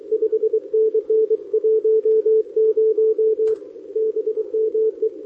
Morse code, heard as a keyed tone, about 24 wpm, 410 Hz, with radio noise. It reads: 5NN 10 BME
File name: OK QRP 2026 / OKQRP2026